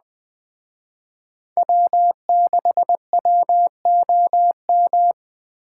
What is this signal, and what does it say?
Morse code, heard as a keyed tone, about 20 wpm, 700 Hz. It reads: W6WOM